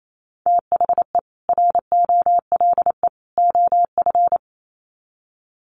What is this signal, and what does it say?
Morse code, heard as a keyed tone, about 28 wpm, 700 Hz. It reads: THE ROLE OF
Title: Sets of 3 Words 28wpm